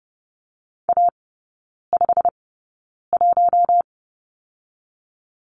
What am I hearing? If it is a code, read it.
Morse code, heard as a keyed tone, about 30 wpm, spaced as for 9 wpm, 700 Hz. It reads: A51
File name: course / 30:10wpm / 087 Lesson 15e - Number 5 - Three Characters 30wpm